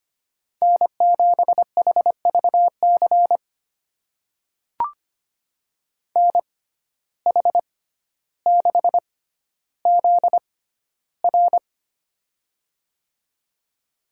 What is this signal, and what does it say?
Morse code, heard as a keyed tone, about 25 wpm, spaced as for 13 wpm, 700 Hz. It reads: N7HVC NH6ZR
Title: USA Call Signs - 5 Characters - 6x Character Spacing - 25wpm